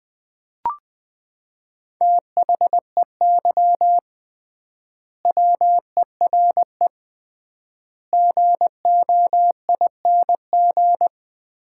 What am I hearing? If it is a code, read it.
Morse code, heard as a keyed tone, about 20 wpm, 700 Hz. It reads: THEY WERE GOING
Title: 3x Word-Spacing - Sets of 4 Words - 20wpm